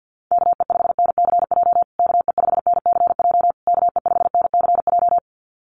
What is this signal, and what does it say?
Morse code, heard as a keyed tone, about 50 wpm, 700 Hz. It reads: KE5NCJ KE5NCJ KE5NCJ
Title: USA Call Signs Repeated 3x 50wpm